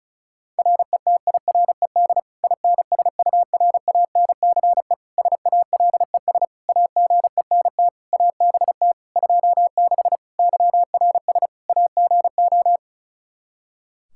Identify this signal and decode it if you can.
Morse code, heard as a keyed tone, about 34 wpm, 700 Hz. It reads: RETIRED INSURANCE SALES AGENT ABT 26 YRS AGO
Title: Practice QSO - Exchange 3 - More Info - 35wpm